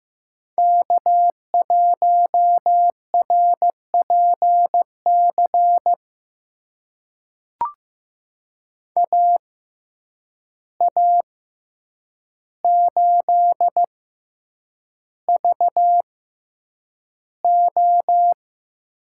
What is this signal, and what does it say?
Morse code, heard as a keyed tone, about 15 wpm, spaced as for 8 wpm, 700 Hz. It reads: K1RPC AA8VO
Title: USA Call Signs - 5 Characters - 6x Character Spacing - 15wpm